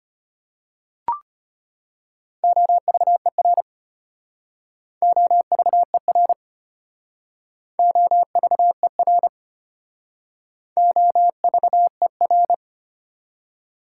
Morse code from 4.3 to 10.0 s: OVER OVER

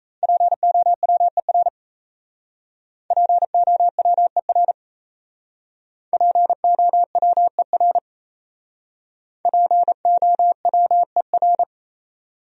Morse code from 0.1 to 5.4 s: POWER POWER

